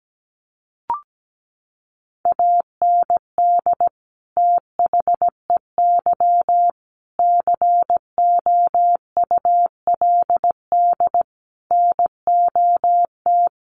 Morse code, heard as a keyed tone, about 17 wpm, 700 Hz. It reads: AND THEY COULD NOT
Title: Sentences from Top 1100 Words 17wpm